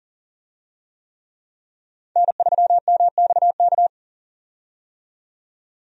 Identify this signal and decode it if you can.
Morse code, heard as a keyed tone, about 39 wpm, 700 Hz. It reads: N3MXK
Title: USA Call Signs - 5 Characters - 40wpm